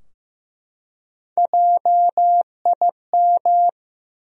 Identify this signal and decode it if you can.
Morse code, heard as a keyed tone, about 15 wpm, 700 Hz. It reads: JIM
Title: USA Common Names - Max 4 Letters - 15wpm